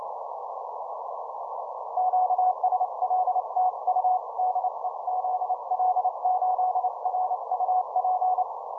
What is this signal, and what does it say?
Morse code, heard as a keyed tone, about 28 wpm, 740 Hz. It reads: QSL TU DE ZL8R UP
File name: simple-file-list / ZL8R 15M CW